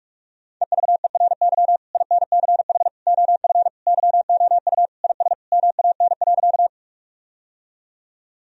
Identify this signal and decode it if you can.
Morse code, heard as a keyed tone, about 46 wpm, 700 Hz. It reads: EVERY INCH OF YOU IS MAN.
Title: Sentences from Top 1000 Words - Review Part 3 - 45wpm